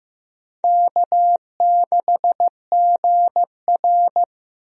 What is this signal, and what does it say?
Morse code, heard as a keyed tone, about 15 wpm, 700 Hz. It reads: K6GR
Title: USA Call Signs - 4 Characters - Repeated 3x - 15wpm